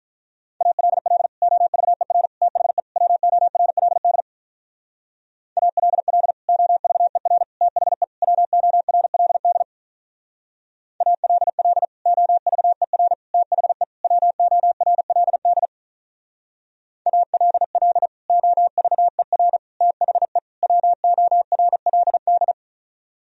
Morse code, heard as a keyed tone, about 43 wpm, 700 Hz. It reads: ALL OVER THE WORLD ALL OVER THE WORLD ALL OVER THE WORLD ALL OVER THE WORLD
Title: Speed-Racing - Sets of 4 Words 35wpm